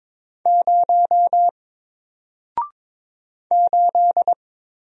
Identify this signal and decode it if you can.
Morse code, heard as a keyed tone, about 22 wpm, 700 Hz. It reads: 0 8
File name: Single-Letter-Number-22wpm